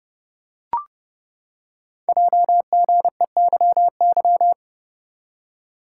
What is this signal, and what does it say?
Morse code, heard as a keyed tone, about 30 wpm, 700 Hz. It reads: JGEYY
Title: Code Groups - 5 Letters - 30wpm